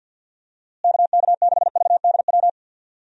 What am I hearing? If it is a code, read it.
Morse code, heard as a keyed tone, about 50 wpm, 700 Hz. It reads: KK6VDW
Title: USA Call Signs - 6 Characters - Repeated 3x - 50wpm